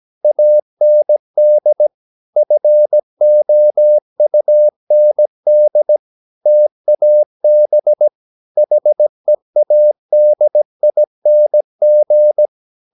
Morse code, heard as a keyed tone, about 17 wpm, 600 Hz. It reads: AND FOUND TAB HEADING
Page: Romanen Never af Ken Follet på engelsk.